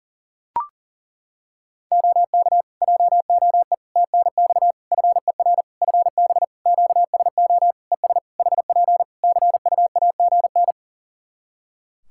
Morse code, heard as a keyed tone, about 37 wpm, 700 Hz. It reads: OK JOE TNX FER FB QSO ES HP CUAGN
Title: K1USN SST QSO Practice 40wpm